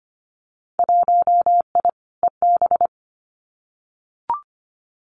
Morse code, heard as a keyed tone, about 25 wpm, 700 Hz. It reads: 1I EB
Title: Winter Field Day Exchanges Encore 25wpm